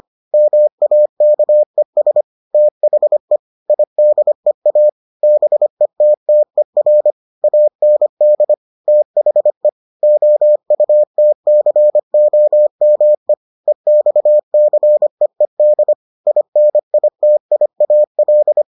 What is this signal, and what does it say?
Morse code, heard as a keyed tone, about 25 wpm, 600 Hz. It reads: MAKES THE IDEA BETTER AND THE OUTCOME EXCEED INITIAL